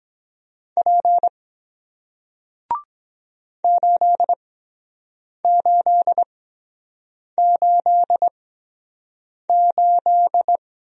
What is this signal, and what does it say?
Morse code, heard as a keyed tone, about 23 wpm, spaced as for 7 wpm, 700 Hz. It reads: P 8888